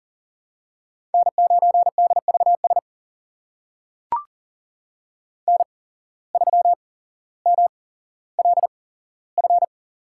Morse code, heard as a keyed tone, about 40 wpm, spaced as for 17 wpm, 700 Hz. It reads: N9DVS N3MLF
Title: USA Call Signs - 5 Characters - 8x Character Spacing - 40wpm